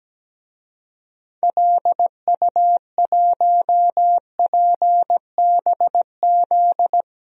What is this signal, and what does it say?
Morse code, heard as a keyed tone, about 17 wpm, 700 Hz. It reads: LU1PBZ
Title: DXCC-100 Call Signs - 6 Characters - Repeated 2x - 17wpm